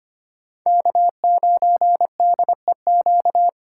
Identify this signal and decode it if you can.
Morse code, heard as a keyed tone, about 25 wpm, 700 Hz. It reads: K9DEQ
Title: USA Call Signs - 5 Characters - 4x Character Spacing - 25wpm